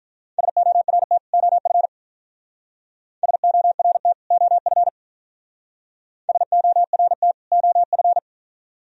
Morse code, heard as a keyed tone, about 42 wpm, 700 Hz. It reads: SORT OF SORT OF SORT OF
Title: Speed-Racing - Sets of 2 Words 35wpm